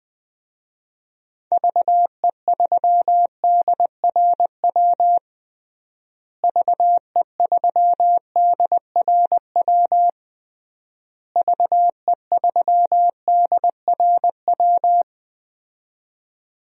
Morse code, heard as a keyed tone, about 20 wpm, 700 Hz. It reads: VE3DRW VE3DRW VE3DRW
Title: DXCC-100 Call Signs - 6 Characters - Repeated 3x - 20wpm